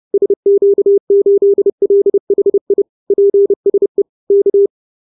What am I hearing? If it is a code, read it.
Morse code, heard as a keyed tone, about 30 wpm, 400 Hz. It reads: SQ8LHI PSE K